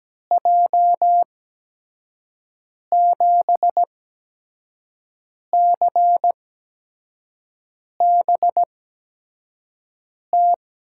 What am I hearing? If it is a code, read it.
Morse code, heard as a keyed tone, about 17 wpm, spaced as for 5 wpm, 700 Hz. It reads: J7CBT
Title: USA Call Signs - 6 Characters - 8x Character Spacing - 17wpm